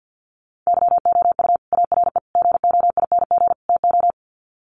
Morse code, heard as a keyed tone, about 51 wpm, 700 Hz. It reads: YOU ARE GOING TO
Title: Sets of 4 Words 50wpm